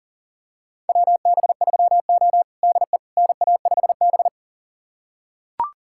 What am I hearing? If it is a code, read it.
Morse code, heard as a keyed tone, about 40 wpm, 700 Hz. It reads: WB3O DE NA5B
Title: USA Callsign DE Callsign - 4 Characters - 40wpm